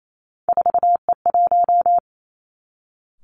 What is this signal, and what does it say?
Morse code, heard as a keyed tone, about 28 wpm, 700 Hz. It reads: 4E1
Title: Three Challenging Letter-Number Combinations 28wpm